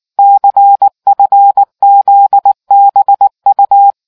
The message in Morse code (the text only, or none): CFZBU